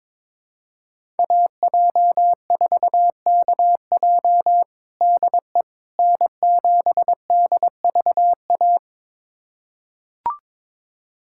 Morse code, heard as a keyed tone, about 22 wpm, 700 Hz. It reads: AJ4KJ DE N7DVA
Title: USA Callsign DE Callsign - 5 Characters - 22wpm